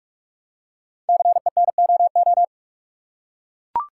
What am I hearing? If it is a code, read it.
Morse code, heard as a keyed tone, about 44 wpm, 700 Hz. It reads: CENOO
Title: Code Groups - 5 Letters - 45wpm